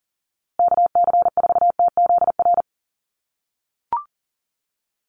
Morse code, heard as a keyed tone, about 40 wpm, 700 Hz. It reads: KC4TZR
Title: USA Call Signs Repeated 3x 40wpm